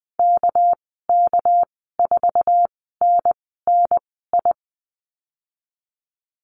Morse code, heard as a keyed tone, about 19 wpm, spaced as for 14 wpm, 700 Hz. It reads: KK4NNI
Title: 200 Lesson 32g - Letter k - Call Signs 20wpm